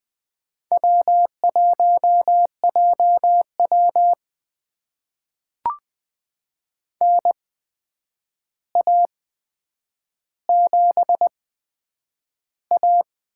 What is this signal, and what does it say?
Morse code, heard as a keyed tone, about 20 wpm, spaced as for 9 wpm, 700 Hz. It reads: W1JW NA7A
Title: USA Call Signs - 4 Characters - 8x Character Spacing - 20wpm